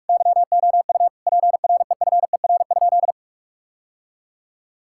Morse code, heard as a keyed tone, about 41 wpm, 700 Hz. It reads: YOU PREFER?
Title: Sentences from Top 1600 Words 45wpm